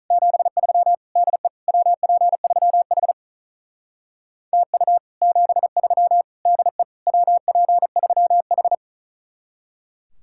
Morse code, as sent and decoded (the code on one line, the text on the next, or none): --... ...-- / -.. / .-- .--. ...-- .... / - ..- / --... ...-- / -.. / .-- .--. ...-- ....
73 DE WP3H TU 73 DE WP3H